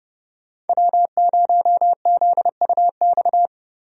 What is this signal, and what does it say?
Morse code, heard as a keyed tone, about 31 wpm, 700 Hz. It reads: W0ZUX
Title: USA Call Signs - 5 Characters - Repeated 2x - 30wpm